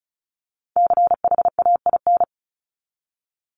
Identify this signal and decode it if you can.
Morse code, heard as a keyed tone, about 34 wpm, 700 Hz. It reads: CHAIN
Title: Top 1251-1500 Words 35wpm